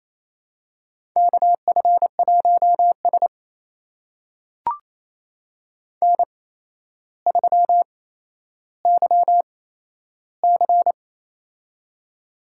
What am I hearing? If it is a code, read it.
Morse code, heard as a keyed tone, about 28 wpm, spaced as for 12 wpm, 700 Hz. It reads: KF1S N3YC
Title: USA Call Signs - 4 Characters - 8x Character Spacing - 28wpm